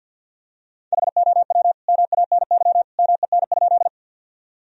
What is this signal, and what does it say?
Morse code, heard as a keyed tone, about 50 wpm, 700 Hz. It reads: HOW MANY MEN?
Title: Sentences from Top 1000 Words - Review Part 1 - 50wpm